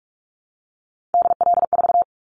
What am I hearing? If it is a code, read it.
Morse code, heard as a keyed tone, about 46 wpm, 700 Hz. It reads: DL4
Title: Callsign Prefixes for Top 25 DXCC - 45wpm